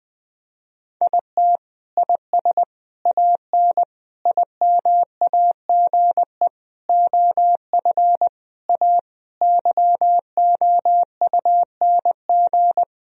Morse code, heard as a keyed tone, about 21 wpm, 700 Hz. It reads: IT IS AN IMAGE OF A YOUNG
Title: Sentences from Top 1000 Words - Review Part 4 - 20wpm